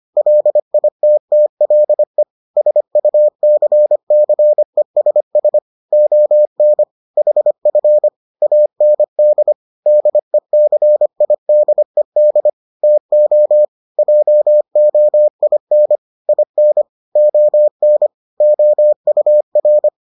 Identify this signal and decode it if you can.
Morse code, heard as a keyed tone, about 25 wpm, 600 Hz. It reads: LITTLE SUCCESS ON HF AND DECIDED TO JOIN IN ON OUR